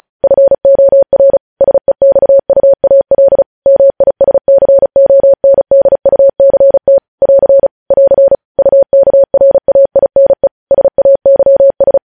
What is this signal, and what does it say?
Morse code, heard as a keyed tone, about 36 wpm, 550 Hz. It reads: FOR SEXUAL MISCONDUCT <AR> <AR> UKRAINE SAYS
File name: podcast / News headlines-1x-35x35